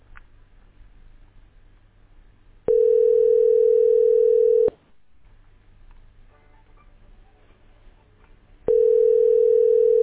Ringback tone (North America), 440 + 480 Hz.
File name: 50efe905-6a87-4075-8662